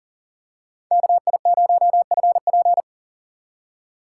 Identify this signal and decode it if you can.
Morse code, heard as a keyed tone, about 39 wpm, 700 Hz. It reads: KI0FP
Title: USA Call Signs - 8x Character Spacing - 40wpm